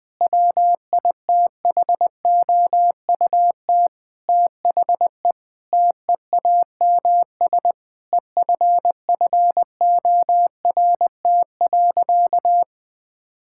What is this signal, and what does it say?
Morse code, heard as a keyed tone, about 21 wpm, 700 Hz. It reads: WITHOUT THE TEAMS EFFORT.
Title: Sentences with 7-Word Phrases - Spoken Beginning - 20wpm